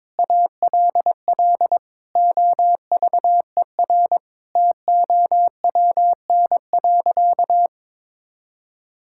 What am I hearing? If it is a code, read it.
Morse code, heard as a keyed tone, about 22 wpm, 700 Hz. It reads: ALL OVER TOWN.
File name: Difficult Sentences with Hints - 22wpm